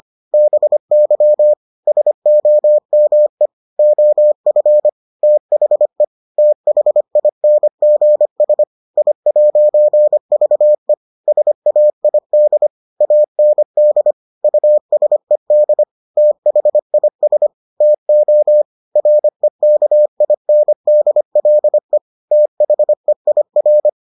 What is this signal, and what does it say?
Morse code, heard as a keyed tone, about 25 wpm, 600 Hz. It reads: BY SOME OF THE THINGS I'VE SAID AND USED THIS TO REKINDLE THEIR